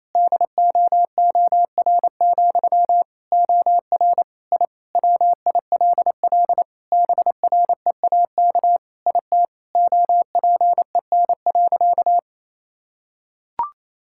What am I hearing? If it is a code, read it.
Morse code, heard as a keyed tone, about 28 wpm, 700 Hz. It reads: DOOR, OR I WILL BREAK IT OPEN.
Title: Sentences from Top 1000 Words - Review Part 6 - 28wpm